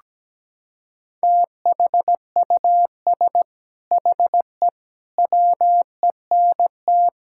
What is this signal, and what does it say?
Morse code, heard as a keyed tone, about 17 wpm, 700 Hz. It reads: THUS HE WENT
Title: Sentences from Top 1000 Words - Review Part 5 - 17wpm